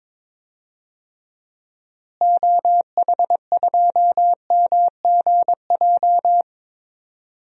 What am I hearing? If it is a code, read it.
Morse code, heard as a keyed tone, about 22 wpm, 700 Hz. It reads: OH2MGJ
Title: DXCC-100 Call Signs - 6 Characters - Repeated 3x - 22wpm